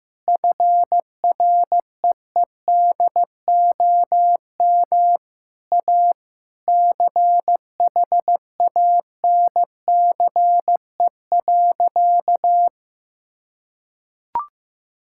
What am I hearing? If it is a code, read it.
Morse code, heard as a keyed tone, about 15 wpm, 700 Hz. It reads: FREEDOM A CHANCE.
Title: Sentences from Top 1000 Words - Review Part 5 - 15wpm